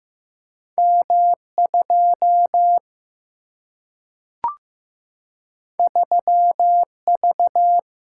Morse code, heard as a keyed tone, about 15 wpm, 700 Hz. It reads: M 2 3 V
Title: Single-Letter Single-Number 15wpm